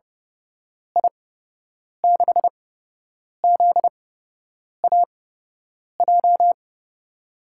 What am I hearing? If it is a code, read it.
Morse code, heard as a keyed tone, about 30 wpm, spaced as for 8 wpm, 700 Hz. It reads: I6ZAJ
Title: USA Call Signs - 6 Characters - 8x Character Spacing - 30wpm